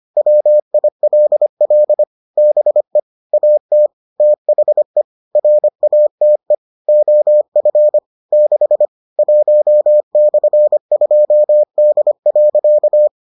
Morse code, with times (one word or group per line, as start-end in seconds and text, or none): 0.2-2.0 WILL
2.4-3.0 BE
3.3-3.9 AT
4.2-5.0 THE
5.4-6.5 RATE
6.9-8.0 OF
8.3-8.9 6
9.2-13.1 1/2D.